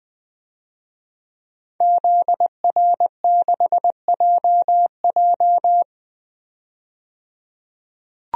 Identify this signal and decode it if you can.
Morse code, heard as a keyed tone, about 21 wpm, 700 Hz. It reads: ZR6JJ
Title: DXCC-100 Call Signs - 5 Characters - 20wpm